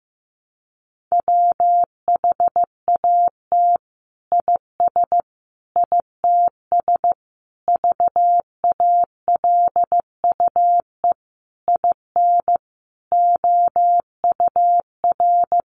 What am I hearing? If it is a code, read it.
Morse code, heard as a keyed tone, about 15 wpm, 700 Hz. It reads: WHAT IS ITS VALUE IN OUR
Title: Sentences from Top 600 Words 15wpm